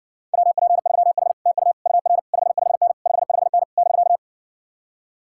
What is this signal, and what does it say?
Morse code, heard as a keyed tone, about 60 wpm, 700 Hz. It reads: WP3H TU UR 55N 55N <BK>